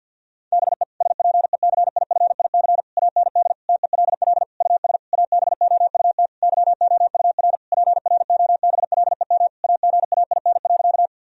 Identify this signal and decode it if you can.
Morse code, heard as a keyed tone, about 52 wpm, 700 Hz. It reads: BE SPECIFIC AND TELL US ABOUT YOUR PROBLEM AGAIN.
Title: Sentences from Top 1000 Words - Review Part 3 - 50wpm